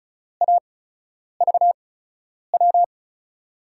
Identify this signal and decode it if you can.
Morse code, heard as a keyed tone, about 36 wpm, 700 Hz. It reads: A V W
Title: USA Call Signs - 5 Characters - 8x Character Spacing - 35wpm